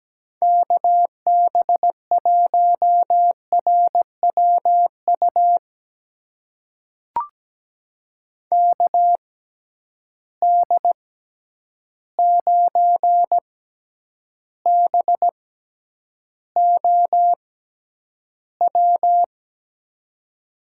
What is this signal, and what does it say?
Morse code, heard as a keyed tone, about 17 wpm, spaced as for 9 wpm, 700 Hz. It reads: KB1RWU KD9BOW
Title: USA Call Signs - 6 Characters - 6x Character Spacing - 17wpm